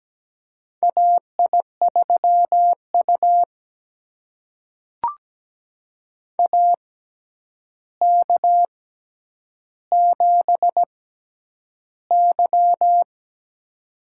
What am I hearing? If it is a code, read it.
Morse code, heard as a keyed tone, about 17 wpm, spaced as for 9 wpm, 700 Hz. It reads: AI3U AK7Y
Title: USA Call Signs - 4 Characters - 6x Character Spacing - 17wpm